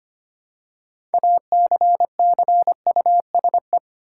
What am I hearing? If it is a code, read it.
Morse code, heard as a keyed tone, about 25 wpm, 700 Hz. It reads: ACCUSE
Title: Six Letter Words - 25wpm